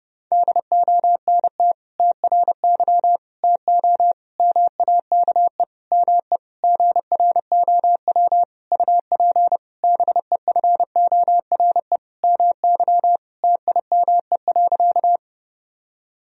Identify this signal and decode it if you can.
Morse code, heard as a keyed tone, about 30 wpm, 700 Hz. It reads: DONT TRY TO MAKE ME GROW UP BEFORE MY TIME.
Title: Sentences from Top 1000 Words - Review Part 1 - 30wpm